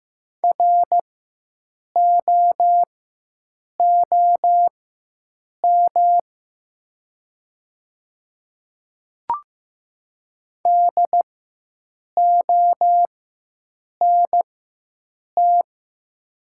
Morse code, heard as a keyed tone, about 15 wpm, spaced as for 7 wpm, 700 Hz. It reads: ROOM DONT
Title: 4x Character Spacing - Top 300 Words - 15wpm